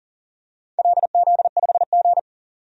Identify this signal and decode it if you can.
Morse code, heard as a keyed tone, about 41 wpm, 690 Hz. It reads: LZ5G
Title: DXCC-100 Call Signs - 4 Characters - 40wpm